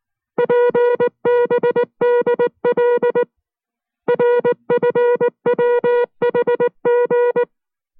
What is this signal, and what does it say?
Morse code, heard as a keyed tone, about 19 wpm, 480 Hz. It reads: PBDL RFWHG